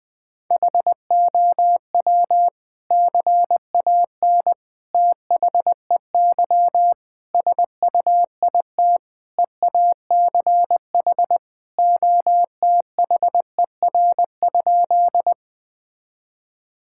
Morse code, heard as a keyed tone, about 20 wpm, 700 Hz. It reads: HOW CAN THEY SUIT EACH OTHER?
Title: Sentences from Top 1600 Words 20wpm